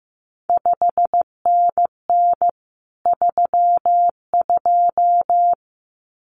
Morse code, heard as a keyed tone, about 15 wpm, 700 Hz. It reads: 5NN 32
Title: CQ WW DX Contest Exchange Encore - 15wpm